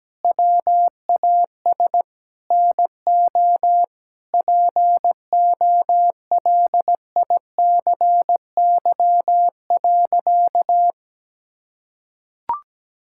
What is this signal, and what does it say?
Morse code, heard as a keyed tone, about 17 wpm, 700 Hz. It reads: WAS NO POLICY.
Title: Sentences from Top 1000 Words - Review Part 5 - 17wpm